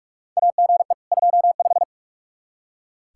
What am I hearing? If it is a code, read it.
Morse code, heard as a keyed tone, about 46 wpm, 700 Hz. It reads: AGE 25
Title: Practice QSO - Exchange 3 - More Info - 45wpm